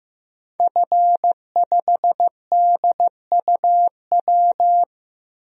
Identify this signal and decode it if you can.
Morse code, heard as a keyed tone, about 16 wpm, 700 Hz. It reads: F5DUW
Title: DXCC-100 Call Signs - 5 Characters - Repeated 2x - 15wpm